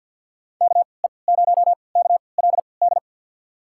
Morse code, heard as a keyed tone, about 51 wpm, spaced as for 28 wpm, 700 Hz. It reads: KE0KLD
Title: USA Call Signs - 3x Character Spacing - 50wpm